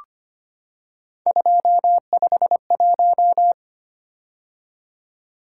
Morse code, heard as a keyed tone, about 25 wpm, 700 Hz. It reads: 251
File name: Three Letter-Number Combinations - 25wpm